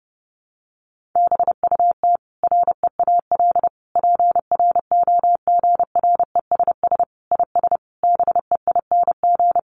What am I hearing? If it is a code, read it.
Morse code, heard as a keyed tone, about 28 wpm, 700 Hz. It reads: BUT REAL PROGRESS IS BEING
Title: Sentences from Top 1400 Words 30wpm